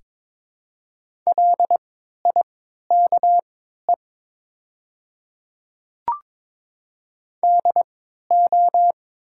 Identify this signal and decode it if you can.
Morse code, heard as a keyed tone, about 22 wpm, spaced as for 12 wpm, 700 Hz. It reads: LIKE DO
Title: 3x Character Spacing - Top 100 Words - 22wpm